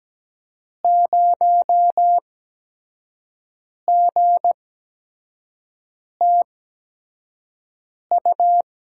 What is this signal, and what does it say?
Morse code, heard as a keyed tone, about 17 wpm, spaced as for 5 wpm, 700 Hz. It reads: 0GTU
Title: USA Call Signs - 6 Characters - 8x Character Spacing - 17wpm